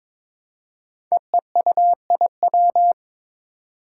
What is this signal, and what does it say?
Morse code, heard as a keyed tone, about 22 wpm, 700 Hz. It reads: EEUIW
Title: Five Challenging Letter-Number Combinations 22wpm